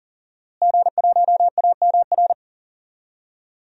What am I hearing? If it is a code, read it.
Morse code, heard as a keyed tone, about 37 wpm, 700 Hz. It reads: G1AMR
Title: DXCC-100 Call Signs - 5 Characters - Repeated 3x - 40wpm